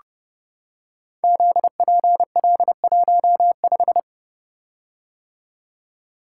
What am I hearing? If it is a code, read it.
Morse code, heard as a keyed tone, about 30 wpm, 700 Hz. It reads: ZPL15
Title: Code Groups - 5 Letter:Numbers - 30wpm